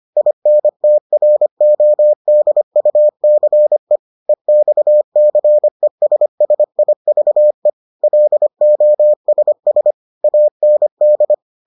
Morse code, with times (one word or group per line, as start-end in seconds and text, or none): 0.2-4.0 INTRODUCE
4.3-7.7 EXCESSIVE
8.0-9.9 LOSS
10.2-11.3 AND